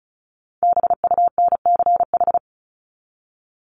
Morse code, heard as a keyed tone, about 34 wpm, 700 Hz. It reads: BUNCH
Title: Top 1751-2000 Words 35wpm